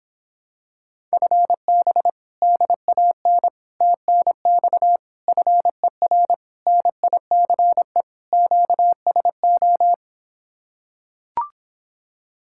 Morse code, heard as a keyed tone, about 26 wpm, 700 Hz. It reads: FB DAN TNX FER NICE QSO